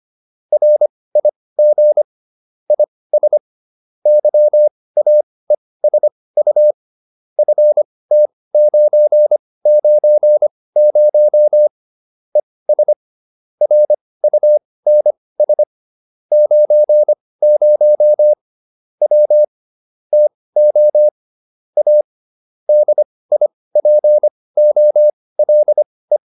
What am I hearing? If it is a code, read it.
Morse code, heard as a keyed tone, about 25 wpm, spaced as for 18 wpm, 600 Hz. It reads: RIG IS YAESU FT990 ES RUNS 90 W TO A DIPOLE
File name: QSO304 18